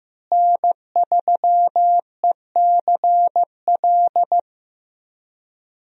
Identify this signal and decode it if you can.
Morse code, heard as a keyed tone, about 15 wpm, 700 Hz. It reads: N3ECL
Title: USA Call Signs - 5 Characters - Repeated 3x - 15wpm